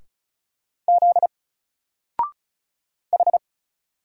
Morse code, heard as a keyed tone, about 36 wpm, 700 Hz. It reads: Z H
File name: Single-Letter-Number-35wpm